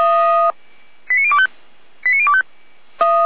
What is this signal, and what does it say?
Five-tone selective call (ZVEI), heard as quick runs of five tones, tones of about 80 ms. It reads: 89025 89025